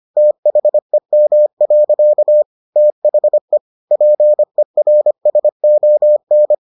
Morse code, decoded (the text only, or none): THEM. THE PERSON